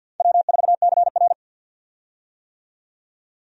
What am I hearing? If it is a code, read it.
Morse code, heard as a keyed tone, about 47 wpm, 700 Hz. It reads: W4CR
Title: USA Call Signs - 4 Characters - 50wpm